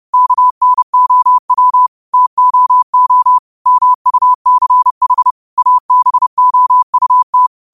Morse code, heard as a keyed tone, about 30 wpm, 1000 Hz. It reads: KNOW TOO MUCH ABOUT
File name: qotdfiles / 30wpm